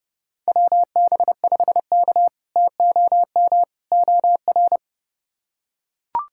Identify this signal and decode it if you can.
Morse code, heard as a keyed tone, about 31 wpm, 700 Hz. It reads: WB5K TOM OR
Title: K1USN SST QSO Practice - 30wpm